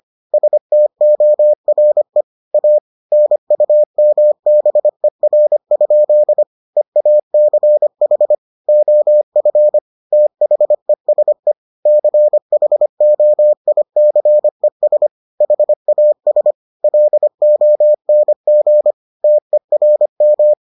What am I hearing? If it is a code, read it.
Morse code, heard as a keyed tone, about 25 wpm, 600 Hz. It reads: STORE A NUMBER? EACH OF THESE CHOICES HAS LONG TERM